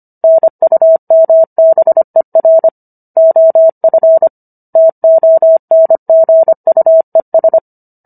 Morse code, heard as a keyed tone, about 25 wpm, 650 Hz. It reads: NUMBER OF TONGUES